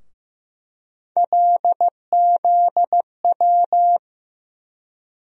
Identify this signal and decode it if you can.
Morse code, heard as a keyed tone, about 15 wpm, 700 Hz. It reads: LZW
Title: USA Call Sign Postfixes - 3 Letters 15wpm